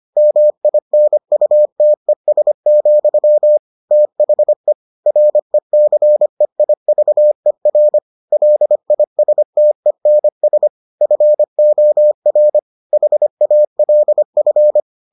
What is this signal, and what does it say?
Morse code, heard as a keyed tone, about 25 wpm, 600 Hz. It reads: MINUTES, THE RECEIVER LISTENS FOR HALF